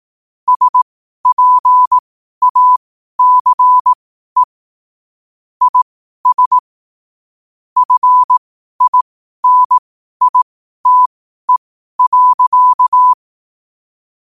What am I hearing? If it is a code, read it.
Morse code, heard as a keyed tone, about 18 wpm, spaced as for 12 wpm, 1000 Hz. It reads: SPACE IS FINITE.
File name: qotdfiles / 12wpm